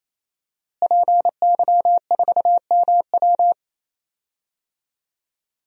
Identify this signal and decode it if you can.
Morse code, heard as a keyed tone, about 27 wpm, 700 Hz. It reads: PY4MW
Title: DXCC-100 Call Signs - 5 Characters - Repeated 3x - 28wpm